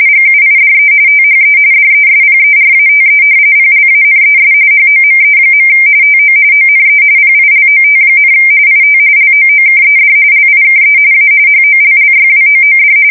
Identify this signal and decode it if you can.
RTTY, about 45.45 baud. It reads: GUIDE THE DESPERATE PLANES HOME. TO ONE PILOT, IT LOOKED LIKE A HOLLYWOOD
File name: fsk170 / battleshipnj